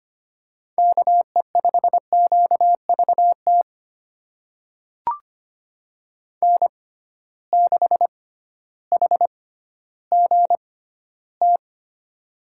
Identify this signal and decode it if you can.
Morse code, heard as a keyed tone, about 25 wpm, spaced as for 13 wpm, 700 Hz. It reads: KE5QVT N6HGT
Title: USA Call Signs - 6x Character Spacing - 25wpm